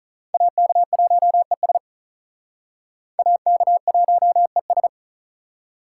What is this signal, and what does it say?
Morse code, heard as a keyed tone, about 43 wpm, 700 Hz. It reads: AK1ES AK1ES
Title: Speed Racing - USA 2x2 Call Signs 35wpm